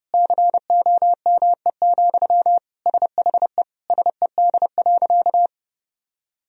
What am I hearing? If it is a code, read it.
Morse code, heard as a keyed tone, about 32 wpm, 700 Hz. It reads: COME, SHE SED.
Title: Sentences from Top 100 Words and Abbreviations 30wpm